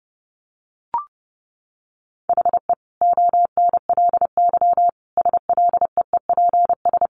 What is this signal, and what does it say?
Morse code, heard as a keyed tone, about 30 wpm, 700 Hz. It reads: HE ONLY SLEEPS
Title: Additional Easy Phrases 30wpm